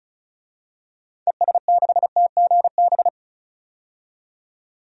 Morse code, heard as a keyed tone, about 34 wpm, 700 Hz. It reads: ES6TGB
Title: DXCC-100 Call Signs - 6 Characters - 35wpm